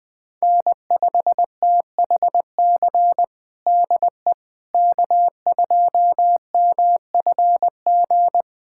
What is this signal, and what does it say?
Morse code, heard as a keyed tone, about 20 wpm, 700 Hz. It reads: N5THC DE K2MFG
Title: USA Callsign DE Callsign - 5 Characters - 20wpm